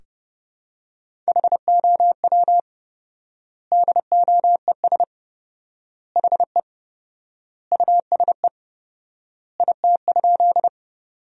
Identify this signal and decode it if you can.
Morse code, heard as a keyed tone, about 30 wpm, 700 Hz. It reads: HOW DOES HE USE IT?
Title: 4x Word-Spacing - Sentences from Top 200 Words - 30wpm